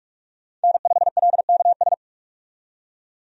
Morse code, heard as a keyed tone, about 46 wpm, 700 Hz. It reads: N5LKS
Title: USA Call Signs - 5 Characters - 45wpm